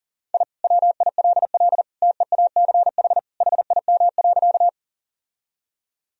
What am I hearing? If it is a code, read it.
Morse code, heard as a keyed tone, about 40 wpm, 700 Hz. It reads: I WILL TEACH HIM.
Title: Sentences from Top 1000 Words - Review Part 2 - 40wpm